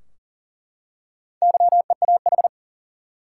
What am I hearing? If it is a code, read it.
Morse code, heard as a keyed tone, about 40 wpm, 700 Hz. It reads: YEAH
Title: Four Letter Words - 40wpm